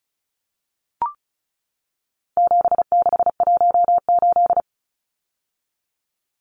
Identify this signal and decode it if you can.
Morse code, heard as a keyed tone, about 34 wpm, 700 Hz. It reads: 7618
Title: Numbers - 4-Digit 35wpm